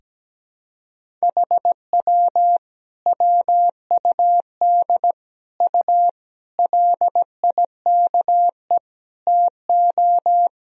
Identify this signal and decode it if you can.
Morse code, heard as a keyed tone, about 17 wpm, 700 Hz. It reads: HW WUD U LIKE TO
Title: Sentences from Top 100 Words and Abbreviations 17wpm